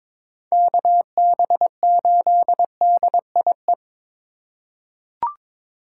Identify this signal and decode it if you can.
Morse code, heard as a keyed tone, about 22 wpm, 700 Hz. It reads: KB8DIE
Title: USA Call Signs - 6 Characters - 3x Character Spacing - 22wpm